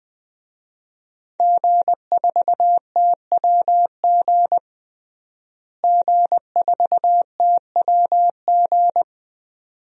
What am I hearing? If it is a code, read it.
Morse code, heard as a keyed tone, about 20 wpm, 700 Hz. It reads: G4TWG G4TWG
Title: DXCC-100 Call Signs - 5 Characters - Repeated 2x - 20wpm